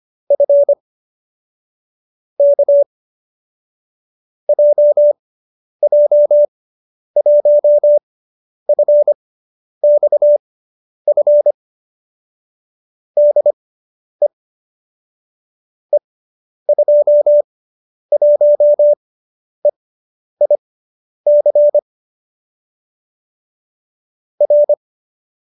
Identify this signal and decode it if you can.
Morse code, heard as a keyed tone, about 25 wpm, spaced as for 10 wpm, 590 Hz. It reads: F K JJ1FXF DE E21EIC R